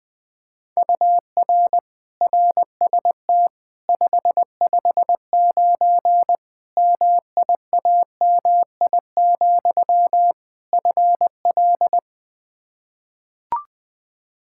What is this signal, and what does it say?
Morse code, heard as a keyed tone, about 20 wpm, 700 Hz. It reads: UR RST 559 MIAMI, FL
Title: K1USN SST QSO Practice 20wpm